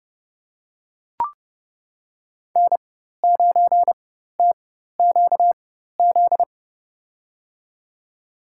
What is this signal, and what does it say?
Morse code, heard as a keyed tone, about 31 wpm, spaced as for 14 wpm, 700 Hz. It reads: N9TQZ
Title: USA Call Signs - 5 Characters - 4x Character Spacing - 30wpm